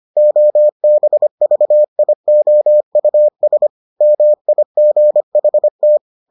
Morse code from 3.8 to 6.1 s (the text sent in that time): MIGHT